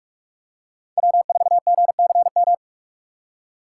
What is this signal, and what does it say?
Morse code, heard as a keyed tone, about 45 wpm, 700 Hz. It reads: W4GCM
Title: USA Call Signs - 4x Character Spacing - 45wpm